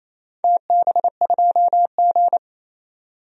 Morse code, heard as a keyed tone, about 28 wpm, 700 Hz. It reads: TB2G
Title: DXCC-100 Call Signs - 4 Characters - 28wpm